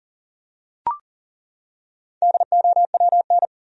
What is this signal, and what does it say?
Morse code, heard as a keyed tone, about 40 wpm, 700 Hz. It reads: DOWN
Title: Top 100 Words 40wpm